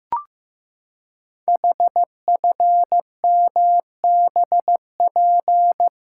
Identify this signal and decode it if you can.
Morse code, heard as a keyed tone, about 15 wpm, 700 Hz. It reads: HFMBP
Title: Code Groups - 5 Letter:Numbers - 15wpm